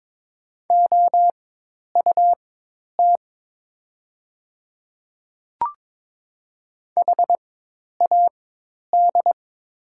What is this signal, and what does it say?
Morse code, heard as a keyed tone, about 22 wpm, spaced as for 10 wpm, 700 Hz. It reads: OUT HAD
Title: 4x Character Spacing - Top 200 Words - 22wpm